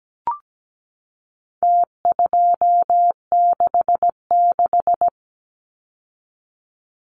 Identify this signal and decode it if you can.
Morse code, heard as a keyed tone, about 17 wpm, 700 Hz. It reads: T266
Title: Numbers - Contest Style - 4-Digit 17wpm